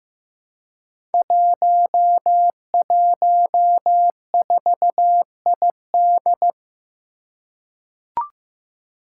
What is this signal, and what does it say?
Morse code, heard as a keyed tone, about 15 wpm, 700 Hz. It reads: 114ID
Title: Five Challenging Letter-Number Combinations 15wpm